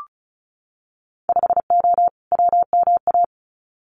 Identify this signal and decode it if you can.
Morse code, heard as a keyed tone, about 36 wpm, 690 Hz. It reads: 5O WMA
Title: Winter Field Day Exchanges 35wpm